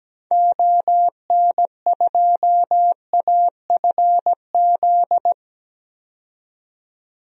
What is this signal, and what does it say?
Morse code, heard as a keyed tone, about 17 wpm, 700 Hz. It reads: ON2AFZ
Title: DXCC-100 Call Signs - 6 Characters - 17wpm